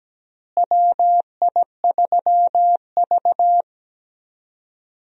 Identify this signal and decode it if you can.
Morse code, heard as a keyed tone, about 17 wpm, 700 Hz. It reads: WI3V
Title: USA Call Signs - 4 Characters - Repeated 2x - 17wpm